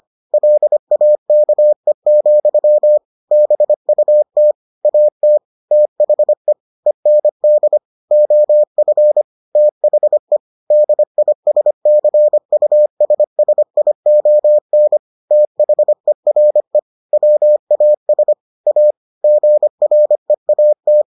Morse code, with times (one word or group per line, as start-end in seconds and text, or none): 0.3-3.0 LAKE,
3.3-4.5 BUT
4.8-5.4 AT
5.7-6.5 THE
6.9-7.8 END
8.1-9.2 OF
9.6-10.4 THE
10.7-15.0 DISCUSSION
15.3-16.8 THERE
17.1-18.3 WAS
18.7-18.9 A
19.2-21.0 GREAT